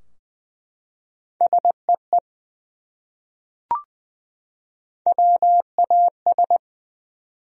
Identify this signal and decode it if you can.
Morse code, heard as a keyed tone, about 20 wpm, 700 Hz. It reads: SEE WAS
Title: Top 100 Words 20wpm